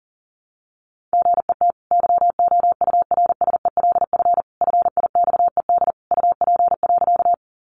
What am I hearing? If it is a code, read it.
Morse code, heard as a keyed tone, about 40 wpm, 700 Hz. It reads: GET YOURSELF FIXED UP.
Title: Sentences from Top 1100 Words 40wpm